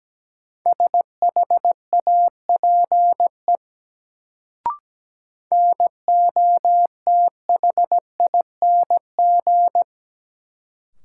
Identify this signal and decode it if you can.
Morse code, heard as a keyed tone, about 17 wpm, 700 Hz. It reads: SHAPE NOTHING
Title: Sentences from Top 500 Words - New Word Drill - 17wpm